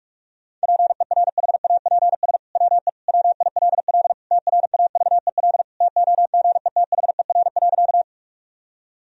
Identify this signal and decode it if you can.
Morse code, heard as a keyed tone, about 46 wpm, 700 Hz. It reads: PERHAPS WE WILL TRAVEL TOGETHER.
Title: Sentences from Top 1000 Words - Review Part 2 - 45wpm